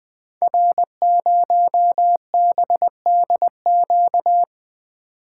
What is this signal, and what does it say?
Morse code, heard as a keyed tone, about 20 wpm, 700 Hz. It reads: R0BDQ
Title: Code Groups - 5 Letter:Numbers - 20wpm